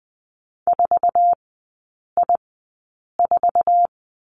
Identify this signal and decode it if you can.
Morse code, heard as a keyed tone, about 20 wpm, spaced as for 8 wpm, 700 Hz. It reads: 4I4
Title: 053 Lesson 10e - Letter r - Three Characters 20wpm